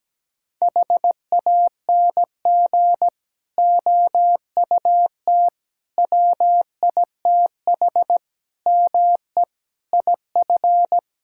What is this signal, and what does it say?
Morse code, heard as a keyed tone, about 17 wpm, 700 Hz. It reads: HANG OUT WITH ME IF
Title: Sentences from Top 1600 Words 17wpm